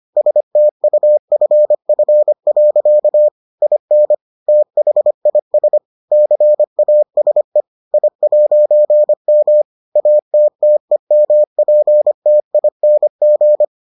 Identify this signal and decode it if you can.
Morse code, heard as a keyed tone, about 25 wpm, 600 Hz. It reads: STUFF. IN THIS CASE I'M ATTEMPTING